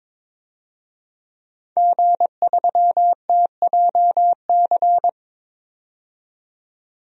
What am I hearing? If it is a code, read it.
Morse code, heard as a keyed tone, about 22 wpm, 700 Hz. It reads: G3TJC
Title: DXCC-100 Call Signs - 5 Characters - 22wpm